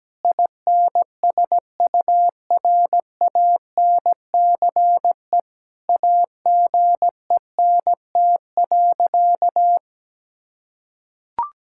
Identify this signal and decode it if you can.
Morse code, heard as a keyed tone, about 17 wpm, 700 Hz. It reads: INSURANCE AGENT.
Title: Sentences from Top 1600 Words 17wpm